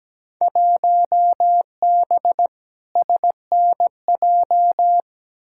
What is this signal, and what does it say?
Morse code, heard as a keyed tone, about 17 wpm, 700 Hz. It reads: 1B SNJ
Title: Field Day Exchanges - 17wpm